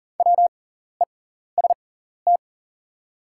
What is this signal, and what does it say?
Morse code, heard as a keyed tone, about 38 wpm, spaced as for 14 wpm, 700 Hz. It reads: WEST
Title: 6x Character Spacing - Top 500 Words - 40wpm